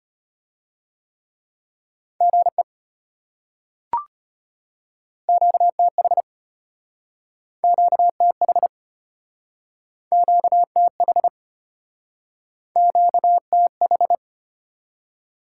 Morse code, heard as a keyed tone, about 33 wpm, 700 Hz. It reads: GE QTH QTH QTH QTH